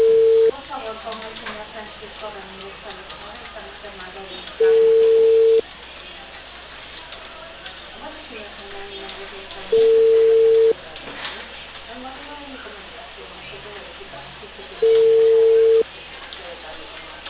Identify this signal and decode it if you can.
Ringback tone (China, 450 Hz), with speech.